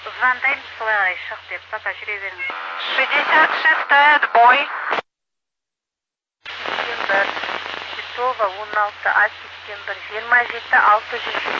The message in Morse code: none

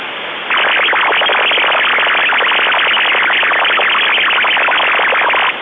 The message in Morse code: none

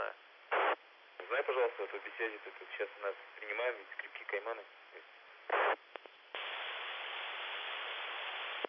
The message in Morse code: none

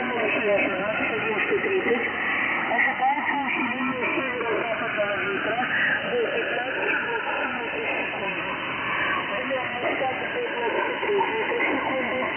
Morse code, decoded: none

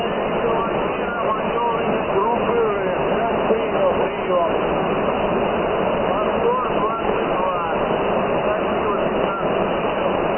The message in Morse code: none

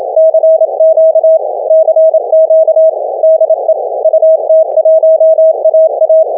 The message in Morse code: CQ CQ DE UT1AA

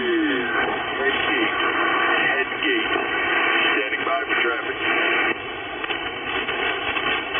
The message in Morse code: none

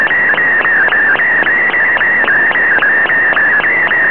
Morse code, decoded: none

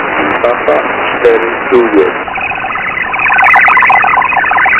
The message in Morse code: none

